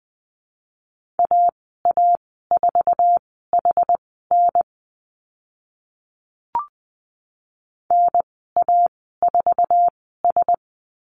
Morse code, heard as a keyed tone, about 20 wpm, spaced as for 14 wpm, 700 Hz. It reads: AA4HN NA4S
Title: 062 Lesson 11g - Letter h - Call Signs 20wpm